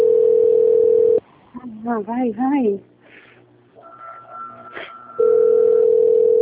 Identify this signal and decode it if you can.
Ringback tone (North America), 440 + 480 Hz, with speech.
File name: out- - - - - .31159